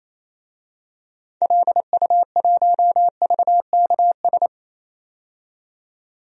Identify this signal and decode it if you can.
Morse code, heard as a keyed tone, about 28 wpm, 700 Hz. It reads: LU1VKS
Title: DXCC-100 Call Signs - 6 Characters - 28wpm